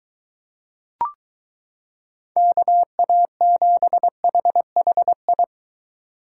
Morse code, heard as a keyed tone, about 22 wpm, 700 Hz. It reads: KA7HHI